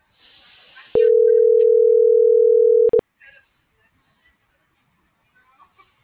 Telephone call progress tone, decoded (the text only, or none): ringback tone